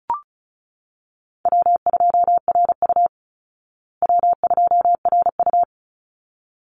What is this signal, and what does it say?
Morse code, heard as a keyed tone, about 34 wpm, 700 Hz. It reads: W2RU W2RU
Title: Call Signs Repeated 3x - 35wpm